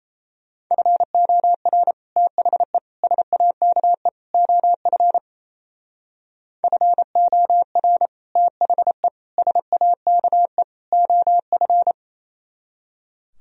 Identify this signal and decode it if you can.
Morse code, heard as a keyed tone, about 35 wpm, 700 Hz. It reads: FOR THE SAKE OF FOR THE SAKE OF